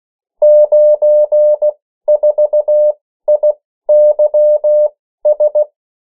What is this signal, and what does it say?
Morse code, heard as a keyed tone, about 16 wpm, spaced as for 13 wpm, 600 Hz. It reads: 94IYS
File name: morse mp3 / 14 urok 13